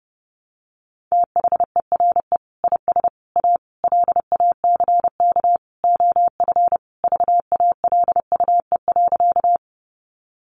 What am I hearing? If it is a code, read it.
Morse code, heard as a keyed tone, about 30 wpm, 700 Hz. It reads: THERE IS A LACK OF VALUE.
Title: Sentences from Top 1100 Words 30wpm